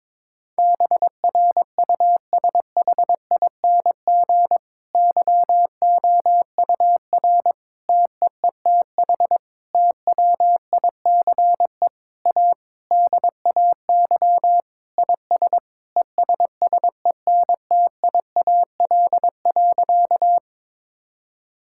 Morse code, heard as a keyed tone, about 22 wpm, 700 Hz. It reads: BRUSHING YOUR TEETH TWICE A DAY IS ESSENTIAL.
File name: Difficult Sentences with Hints - 22wpm